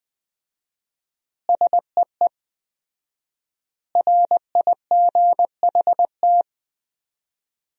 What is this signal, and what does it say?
Morse code, heard as a keyed tone, about 20 wpm, 700 Hz. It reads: SEE RIGHT
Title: 4x Word-Spacing - Sentences from Top 200 Words - 20wpm